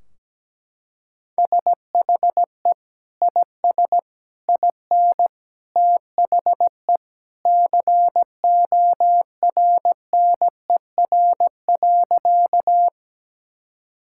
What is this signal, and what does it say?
Morse code, heard as a keyed tone, about 17 wpm, 700 Hz. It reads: SHE IS IN THE CORNER.
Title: Sentences from Top 1600 Words 17wpm